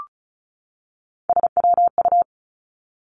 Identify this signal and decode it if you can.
Morse code, heard as a keyed tone, about 36 wpm, 700 Hz. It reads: SWU
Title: Three Challenging Letter-Number Combinations 35wpm